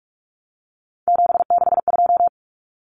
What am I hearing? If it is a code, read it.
Morse code, heard as a keyed tone, about 44 wpm, 700 Hz. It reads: 762
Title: Numbers - 3-Digit 45wpm